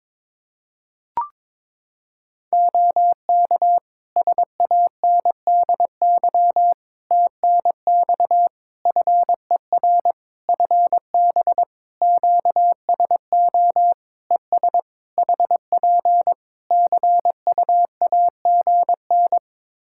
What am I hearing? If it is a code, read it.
Morse code, heard as a keyed tone, about 22 wpm, 700 Hz. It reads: OK SANDY TNX FER FB QSO ES HP CUAGN
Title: K1USN SST QSO Practice 22wpm